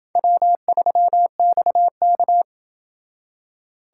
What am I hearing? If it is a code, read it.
Morse code, heard as a keyed tone, about 28 wpm, 700 Hz. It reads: W3XK